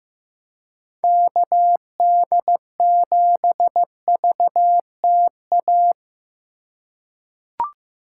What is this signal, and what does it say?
Morse code, heard as a keyed tone, about 15 wpm, 700 Hz. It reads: KD7VTA
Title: USA Call Signs - 3x Character Spacing - 15wpm